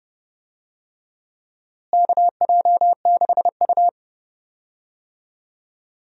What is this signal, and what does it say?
Morse code, heard as a keyed tone, about 30 wpm, 700 Hz. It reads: KJ6U
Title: USA Call Signs - 4 Characters - Repeated 3x - 30wpm